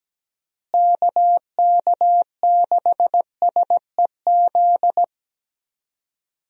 Morse code, heard as a keyed tone, about 17 wpm, 700 Hz. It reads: KK6SEZ
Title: USA Call Signs - 6 Characters - Repeated 2x - 17wpm